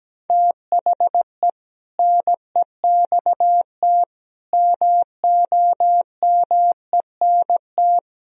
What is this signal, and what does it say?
Morse code, heard as a keyed tone, about 17 wpm, 700 Hz. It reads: THE NEXT MOMENT
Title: Sentences from Top 1000 Words - Review Part 1 - 17wpm